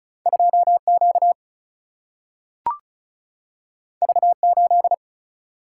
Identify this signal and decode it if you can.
Morse code, heard as a keyed tone, about 34 wpm, 700 Hz. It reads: 2Q V8
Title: Single-Letter Single-Number 35wpm